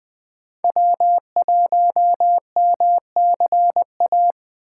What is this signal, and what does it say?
Morse code, heard as a keyed tone, about 20 wpm, 700 Hz. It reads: W1MCA
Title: USA Call Signs - 5 Characters - 6x Character Spacing - 20wpm